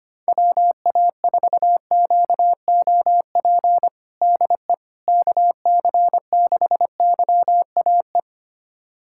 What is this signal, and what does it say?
Morse code, heard as a keyed tone, about 25 wpm, 700 Hz. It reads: WA4QOP DE KC6YAE
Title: USA Callsign DE Callsign 25wpm